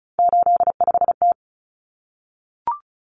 Morse code, heard as a keyed tone, about 34 wpm, 700 Hz. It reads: 85T
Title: Common Cut Numbers - 3-Digit 35wpm